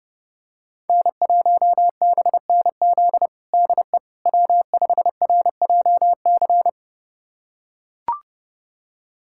Morse code, heard as a keyed tone, about 30 wpm, 700 Hz. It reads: N1BNZ DE W5RJC
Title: USA Callsign DE Callsign - 5 Characters - 30wpm